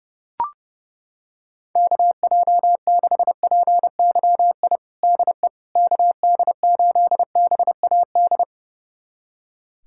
Morse code, heard as a keyed tone, about 30 wpm, 700 Hz. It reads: KJ6PYI DE KD8BAD
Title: USA Callsign DE Callsign - 6 Characters - 30wpm